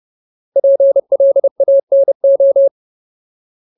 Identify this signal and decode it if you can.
Morse code, heard as a keyed tone, about 30 wpm, 550 Hz. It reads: PLANO
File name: Ultimate Word List / 30WPM Ultimate